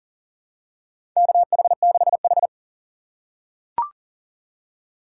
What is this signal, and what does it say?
Morse code, heard as a keyed tone, about 40 wpm, 700 Hz. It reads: KH6H
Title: USA Call Signs - 2x Character Spacing - 40wpm